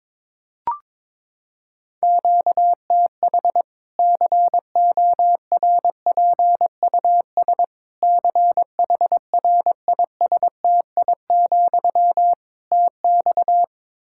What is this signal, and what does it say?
Morse code, heard as a keyed tone, about 22 wpm, 700 Hz. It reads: QTH CORPUS CHRISTI, TX
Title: Practice QSO - Exchange 1 - Beginning - 22wpm